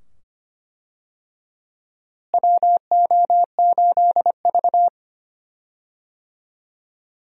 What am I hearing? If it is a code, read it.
Morse code, heard as a keyed tone, about 25 wpm, 700 Hz. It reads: WO8V
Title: USA Call Signs - 4 Characters - Repeated 2x - 25wpm